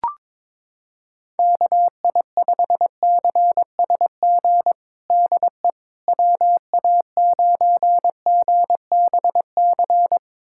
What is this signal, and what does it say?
Morse code, heard as a keyed tone, about 22 wpm, 700 Hz. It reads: KI5CSG DE WA9GBC
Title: USA Callsign DE Callsign - 6 Characters - 22wpm